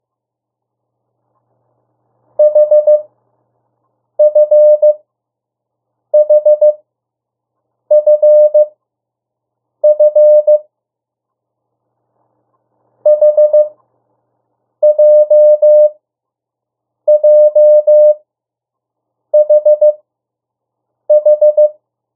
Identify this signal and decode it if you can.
Morse code, heard as a keyed tone, about 15 wpm, spaced as for 6 wpm, 600 Hz. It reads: HFHFF HJJHH
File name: morse mp3 / 02 urok 1